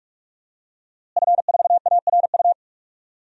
Morse code, heard as a keyed tone, about 46 wpm, 700 Hz. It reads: F4ARU
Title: DXCC-100 Call Signs - 5 Characters - Repeated 3x - 45wpm